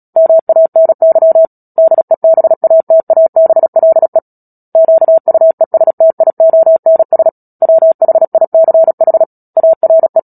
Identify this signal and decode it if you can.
Morse code, heard as a keyed tone, about 37 wpm, 650 Hz. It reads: MANY DEBATABLE QUESTIONS WHICH ARE
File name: Book Two / 10-Chapter 10 - Epilogue - 36 WPM